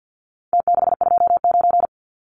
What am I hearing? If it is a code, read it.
Morse code, heard as a keyed tone, about 48 wpm, 700 Hz. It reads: T629
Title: Numbers - Contest Style - 4-Digit 50wpm